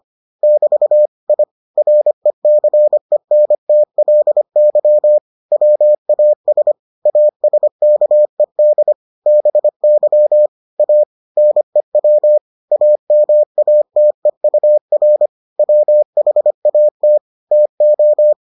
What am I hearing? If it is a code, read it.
Morse code, heard as a keyed tone, about 25 wpm, 600 Hz. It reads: <BT> I RECENTLY WAS ASKED BY A NEW AMATEUR WHAT TO